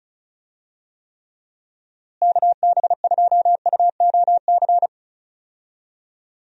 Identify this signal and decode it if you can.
Morse code, heard as a keyed tone, about 36 wpm, 700 Hz. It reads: KB2UOC
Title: USA Call Signs - 6 Characters - Repeated 2x - 35wpm